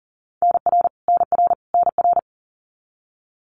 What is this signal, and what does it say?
Morse code, heard as a keyed tone, about 40 wpm, 700 Hz. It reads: NR NR NR
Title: Abbreviations Repeated 3x - 40wpm